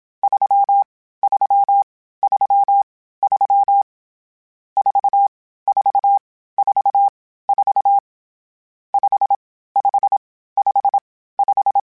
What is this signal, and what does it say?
Morse code, heard as a keyed tone, about 27 wpm, spaced as for 15 wpm, 800 Hz. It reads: 3333 4444 5555
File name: Misc / Misc-mp3s / Numb @25